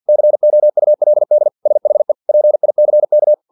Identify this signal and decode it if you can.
Morse code, heard as a keyed tone, about 50 wpm, 600 Hz. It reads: COULD SHE PICK
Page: Never 49wpm